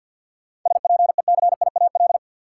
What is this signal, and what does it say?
Morse code, heard as a keyed tone, about 53 wpm, 700 Hz. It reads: SPECIAL
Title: Sentences from Top 1000 Words - Review Part 3 - New Word Drill - 50wpm